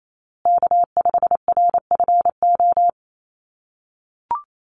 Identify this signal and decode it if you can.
Morse code, heard as a keyed tone, about 28 wpm, 700 Hz. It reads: K5RFO
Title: USA Call Signs Repeated 2x 28wpm